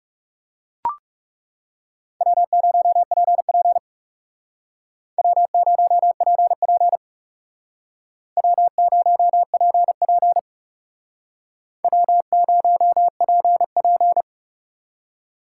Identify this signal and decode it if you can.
Morse code, heard as a keyed tone, about 40 wpm, 700 Hz. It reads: W0PP W0PP W0PP W0PP